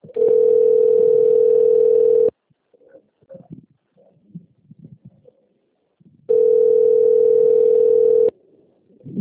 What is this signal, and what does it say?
Ringback tone (North America), 440 + 480 Hz.